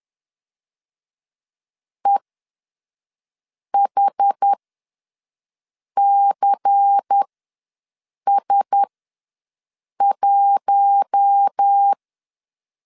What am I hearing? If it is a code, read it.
Morse code, heard as a keyed tone, about 10 wpm, spaced as for 5 wpm, 800 Hz. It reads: EHCS1
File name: LEZIONI / 8 YJ45 / 8 YJ4 ws 10wpm 4020